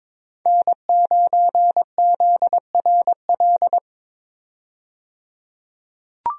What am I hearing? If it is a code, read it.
Morse code, heard as a keyed tone, about 22 wpm, 700 Hz. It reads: N9ZRL
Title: USA Call Signs - 5 Characters - 22wpm